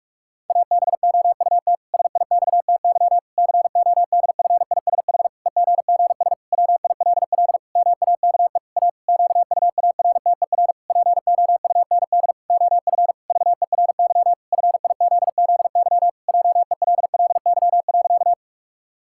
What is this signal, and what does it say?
Morse code, heard as a keyed tone, about 46 wpm, 700 Hz. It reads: ABOUT SIXTY CODFISH EGGS WILL MAKE A QUARTER POUND OF VERY FIZZY JELLY.
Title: Pangram Challenge v2 - 45wpm